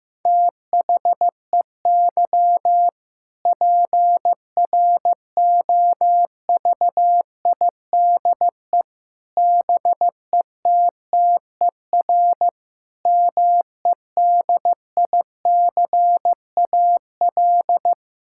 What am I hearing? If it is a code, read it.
Morse code, heard as a keyed tone, about 15 wpm, 700 Hz. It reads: THEY PROVIDE BETTER MEDICAL
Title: Sentences from Top 1000 Words - Review Part 5 - 15wpm